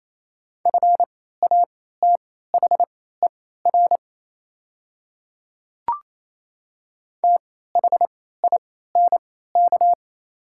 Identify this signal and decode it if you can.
Morse code, heard as a keyed tone, about 28 wpm, spaced as for 16 wpm, 700 Hz. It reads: FATHER THINK
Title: 3x Character Spacing - Top 300 Words - 28wpm